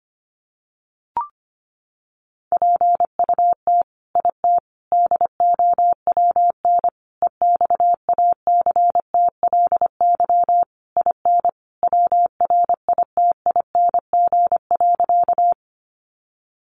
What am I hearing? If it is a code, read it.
Morse code, heard as a keyed tone, about 25 wpm, 700 Hz. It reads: PUT IT DOWN EXACTLY IN WRITING.
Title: Sentences from Top 900 Words 25wpm